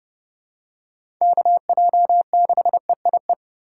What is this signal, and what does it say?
Morse code, heard as a keyed tone, about 30 wpm, 700 Hz. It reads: KJ6EIE
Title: USA Call Signs - 6 Characters - 4x Character Spacing - 30wpm